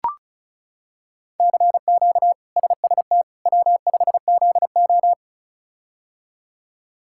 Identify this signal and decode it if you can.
Morse code, heard as a keyed tone, about 34 wpm, 700 Hz. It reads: CQ SST W5ZO
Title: K1USN SST QSO Practice - 35wpm